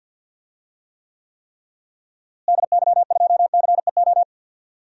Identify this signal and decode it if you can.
Morse code, heard as a keyed tone, about 52 wpm, 700 Hz. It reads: DY2CEO
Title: DXCC-100 Call Signs - 6 Characters - Repeated 3x - 50wpm